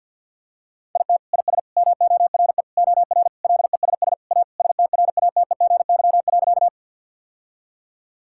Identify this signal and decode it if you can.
Morse code, heard as a keyed tone, about 52 wpm, 700 Hz. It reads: IT IS MORE OR LESS A STRATEGY.
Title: Sentences from Top 1000 Words - Review Part 5 - 50wpm